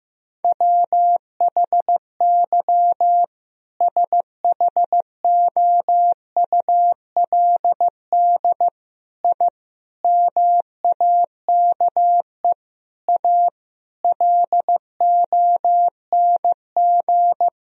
Morse code, heard as a keyed tone, about 15 wpm, 700 Hz. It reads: WHY SHOULD I MAKE A LONG
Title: Sentences from Top 1000 Words - Review Part 1 - 15wpm